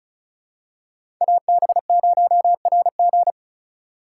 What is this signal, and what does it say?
Morse code, heard as a keyed tone, about 34 wpm, 700 Hz. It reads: AB0RG
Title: USA Call Signs - 5 Characters - 2x Character Spacing - 35wpm